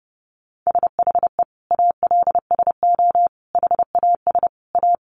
Morse code, heard as a keyed tone, about 30 wpm, 700 Hz. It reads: SHE ALSO HAS A
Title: Easy Phrases 30wpm